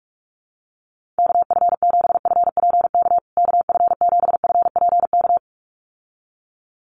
Morse code, heard as a keyed tone, about 44 wpm, 700 Hz. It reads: KF7FPK KF7FPK
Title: USA Call Signs Repeated 2x 45wpm